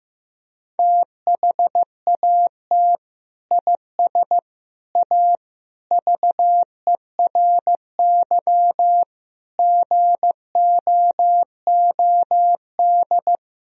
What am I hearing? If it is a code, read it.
Morse code, heard as a keyed tone, about 15 wpm, 700 Hz. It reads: THAT IS A VERY GOOD
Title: Sentences from Top 1000 Words - Review Part 4 - 15wpm